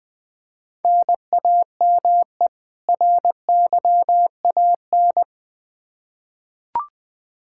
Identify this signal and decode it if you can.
Morse code, heard as a keyed tone, about 20 wpm, 700 Hz. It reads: NAME RYAN
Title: K1USN SST QSO Practice 20wpm